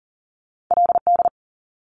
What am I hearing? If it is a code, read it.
Morse code, heard as a keyed tone, about 41 wpm, 700 Hz. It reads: LD
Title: Top Two Letter Combinations in Words 40wpm